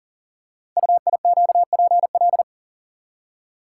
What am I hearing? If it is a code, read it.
Morse code, heard as a keyed tone, about 38 wpm, 690 Hz. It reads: UIQPL
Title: Code Groups - 5 Letters - 40wpm